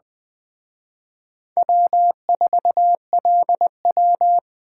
Morse code, heard as a keyed tone, about 21 wpm, 700 Hz. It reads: W4LW